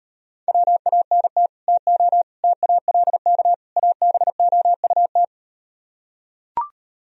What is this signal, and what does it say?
Morse code, heard as a keyed tone, about 38 wpm, 700 Hz. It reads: WANT TO TALK ABOUT